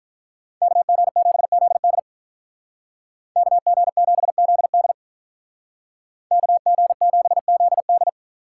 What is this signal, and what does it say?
Morse code, heard as a keyed tone, about 43 wpm, 700 Hz. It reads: KG7ZD KG7ZD KG7ZD
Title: Speed Racing - USA Call Signs 35wpm